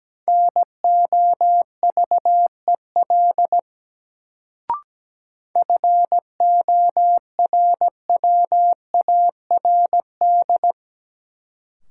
Morse code, heard as a keyed tone, about 17 wpm, 700 Hz. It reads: NOVEL FORWARD
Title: Sentences from Top 1300 Words - New Word Drill Alt - 17wpm